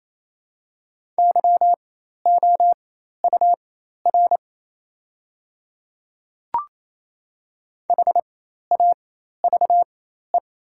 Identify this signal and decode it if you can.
Morse code, heard as a keyed tone, about 28 wpm, spaced as for 13 wpm, 690 Hz. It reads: YOUR HAVE
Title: 4x Character Spacing - Top 100 Words - 28wpm